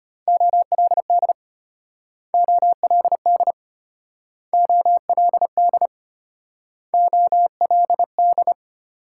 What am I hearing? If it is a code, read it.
Morse code, heard as a keyed tone, about 32 wpm, 700 Hz. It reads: OLD OLD OLD OLD